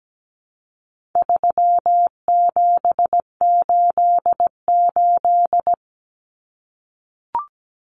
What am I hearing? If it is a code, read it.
Morse code, heard as a keyed tone, about 17 wpm, 700 Hz. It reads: 3788
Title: Common Cut Numbers - 4-Digit 17wpm